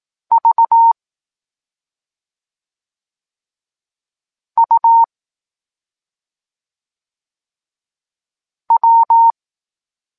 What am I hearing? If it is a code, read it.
Morse code, heard as a keyed tone, about 18 wpm, 920 Hz. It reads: V U W